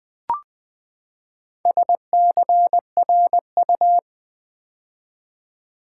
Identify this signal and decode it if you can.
Morse code, heard as a keyed tone, about 20 wpm, 700 Hz. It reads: SCRU
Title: Build A Word - 4 Letters At A Time - 8-Letter Words - 20wpm